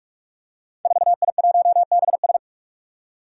Morse code, heard as a keyed tone, about 44 wpm, 700 Hz. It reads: 4I1BS
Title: Five Challenging Letter-Number Combinations 45wpm